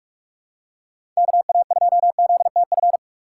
Morse code, heard as a keyed tone, about 48 wpm, 700 Hz. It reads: KA2ZTF
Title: USA Call Signs - 6 Characters - 6x Character Spacing - 45wpm